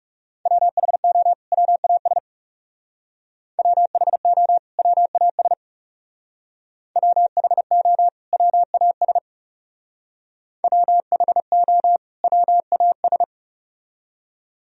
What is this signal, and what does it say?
Morse code, heard as a keyed tone, about 38 wpm, 700 Hz. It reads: WHO WAS WHO WAS WHO WAS WHO WAS
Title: Speed-Racing - Sets of 2 Words 30wpm